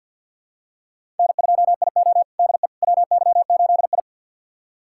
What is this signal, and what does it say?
Morse code, heard as a keyed tone, about 53 wpm, 700 Hz. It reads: N2IO DE WY8I
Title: USA Callsign DE Callsign - 4 Characters - 50wpm